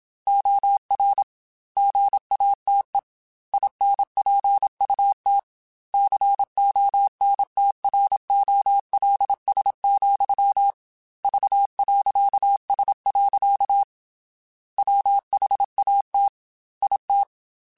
Morse code, heard as a keyed tone, about 27 wpm, 800 Hz. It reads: OR GATE INPUT CONTROLS, V.S. WHAT IT